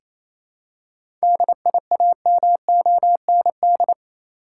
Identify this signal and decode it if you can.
Morse code, heard as a keyed tone, about 27 wpm, 700 Hz. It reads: DIAMOND
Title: Seven Letter Words - 28wpm